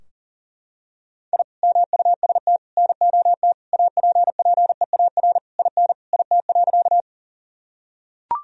Morse code, heard as a keyed tone, about 37 wpm, 700 Hz. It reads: I MUST NOT APPEAR IN IT.
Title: Sentences from Top 1000 Words - Review Part 2 - 40wpm